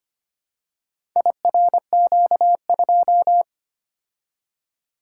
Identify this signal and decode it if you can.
Morse code, heard as a keyed tone, about 25 wpm, 700 Hz. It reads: IRQ2
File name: practice / 25wpm / Four Letter-Number Combinations - 25wpm